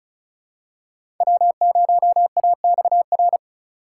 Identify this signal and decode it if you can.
Morse code, heard as a keyed tone, about 36 wpm, 700 Hz. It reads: W0AXR
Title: USA Call Signs - 4x Character Spacing - 35wpm